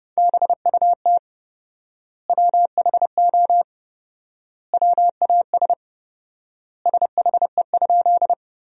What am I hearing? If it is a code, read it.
Morse code, heard as a keyed tone, about 32 wpm, 700 Hz. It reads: BUT WHO WAS SHE?
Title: 4x Word-Spacing - Sentences from Top 100 Words - 30wpm